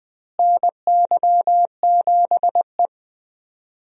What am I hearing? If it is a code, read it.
Morse code, heard as a keyed tone, about 20 wpm, 700 Hz. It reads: NY7E
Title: USA Call Signs - 4 Characters - Repeated 2x - 20wpm